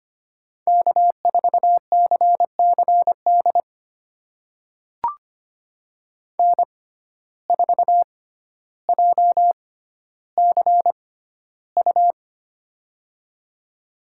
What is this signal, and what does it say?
Morse code, heard as a keyed tone, about 25 wpm, spaced as for 13 wpm, 700 Hz. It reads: K4CCD N4JCU
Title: USA Call Signs - 5 Characters - 6x Character Spacing - 25wpm